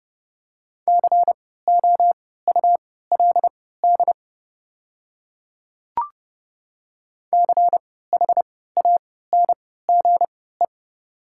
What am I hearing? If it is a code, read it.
Morse code, heard as a keyed tone, about 30 wpm, spaced as for 17 wpm, 700 Hz. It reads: COULD CHANGE
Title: 3x Character Spacing - Top 500 Words - 30wpm